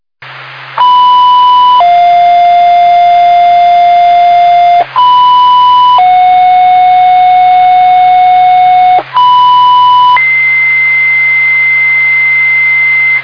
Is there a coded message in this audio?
Yes: two-tone page from 0.8 to 13.2 s, heard as a one-second tone, then a three-second tone, about 990 then 700 Hz, about 990 then 730 Hz, about 990 then 2050 Hz.